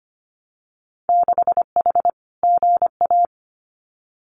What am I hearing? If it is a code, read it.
Morse code, heard as a keyed tone, about 25 wpm, 700 Hz. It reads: 6H GA
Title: Winter Field Day Exchanges Encore 25wpm